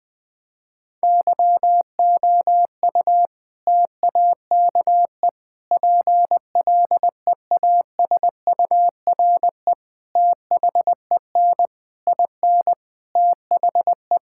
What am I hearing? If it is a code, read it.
Morse code, heard as a keyed tone, about 20 wpm, 700 Hz. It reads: YOU TAKE PLEASURE THEN IN THE
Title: Sentences from Top 1600 Words 20wpm